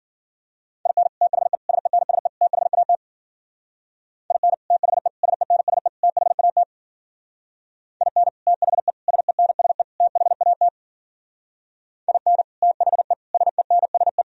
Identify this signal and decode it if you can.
Morse code, heard as a keyed tone, about 52 wpm, 700 Hz. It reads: IN THE SENSE THAT IN THE SENSE THAT IN THE SENSE THAT IN THE SENSE